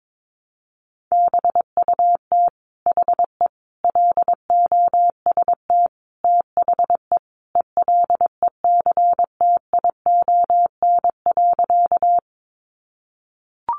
Morse code, heard as a keyed tone, about 22 wpm, 700 Hz. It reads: BUT HE LOST THE ELECTION.
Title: Sentences from Top 1500 Words 22wpm